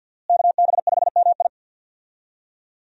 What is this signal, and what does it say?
Morse code, heard as a keyed tone, about 48 wpm, 700 Hz. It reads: KB5MI
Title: USA Call Signs - 5 Characters - Repeated 3x - 50wpm